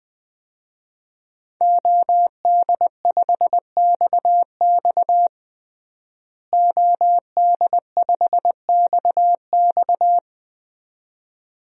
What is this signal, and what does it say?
Morse code, heard as a keyed tone, about 20 wpm, 700 Hz. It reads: OD5XX OD5XX
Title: DXCC-100 Call Signs - 5 Characters - Repeated 2x - 20wpm